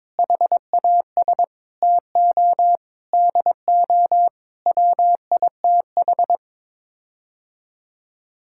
Morse code, heard as a keyed tone, about 21 wpm, 700 Hz. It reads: HAS TO DO WITH
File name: practice / 22wpm / Sets of 8 Words - 22wpm